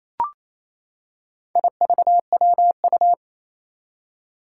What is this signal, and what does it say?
Morse code, heard as a keyed tone, about 28 wpm, 700 Hz. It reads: IVWU
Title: Four Challenging Letter-Number Combinations 28wpm